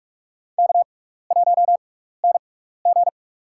Morse code, heard as a keyed tone, about 46 wpm, spaced as for 15 wpm, 700 Hz. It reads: K1NG
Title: USA Call Signs - 6x Character Spacing - 45wpm